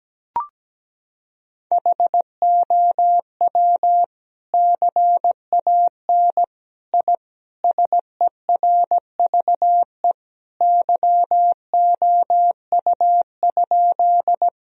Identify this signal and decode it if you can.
Morse code, heard as a keyed tone, about 17 wpm, 700 Hz. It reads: HOW CAN I SERVE YOU?
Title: Sentences from Top 1000 Words - Review Part 2 - 17wpm